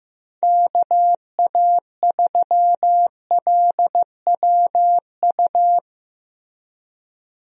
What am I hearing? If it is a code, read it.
Morse code, heard as a keyed tone, about 16 wpm, 700 Hz. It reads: KA3LWU
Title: USA Call Signs - 6 Characters - Repeated 2x - 15wpm